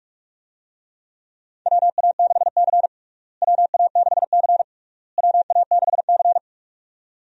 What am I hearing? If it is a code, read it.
Morse code, heard as a keyed tone, about 46 wpm, 700 Hz. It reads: WA6C WA6C WA6C
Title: USA Call Signs - 4 Characters - Repeated 3x - 45wpm